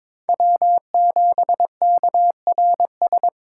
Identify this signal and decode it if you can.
Morse code, heard as a keyed tone, about 22 wpm, 700 Hz. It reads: W7KRS
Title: K1USN SST QSO Practice - 22wpm